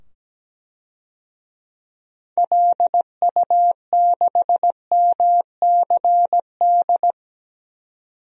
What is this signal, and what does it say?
Morse code, heard as a keyed tone, about 17 wpm, 700 Hz. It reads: LU6MCD
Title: DXCC-100 Call Signs - 6 Characters - 17wpm